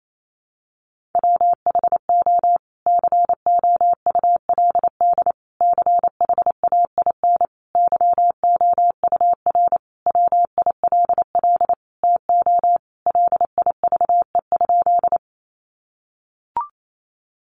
Morse code, heard as a keyed tone, about 28 wpm, 700 Hz. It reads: WHO COULD CHAIN YOUR WILL TO LIVE?
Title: Sentences from Top 1500 Words 28wpm